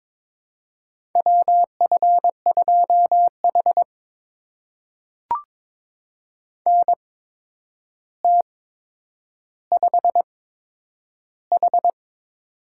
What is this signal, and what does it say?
Morse code, heard as a keyed tone, about 22 wpm, spaced as for 9 wpm, 700 Hz. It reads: WF2H NT5H
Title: USA Call Signs - 4 Characters - 8x Character Spacing - 22wpm